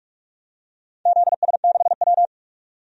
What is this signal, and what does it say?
Morse code, heard as a keyed tone, about 44 wpm, 700 Hz. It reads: ZS6W
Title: DXCC-100 Call Signs - 4 Characters - 45wpm